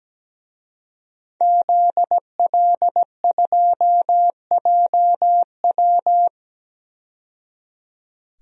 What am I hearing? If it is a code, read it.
Morse code, heard as a keyed tone, about 17 wpm, 700 Hz. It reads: ZL2JW
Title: DXCC-100 Call Signs - 5 Characters - 17wpm